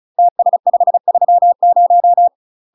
Morse code, heard as a keyed tone, about 37 wpm, 700 Hz. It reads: TS530
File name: UK MASTER QSOs TRAINING/35WPM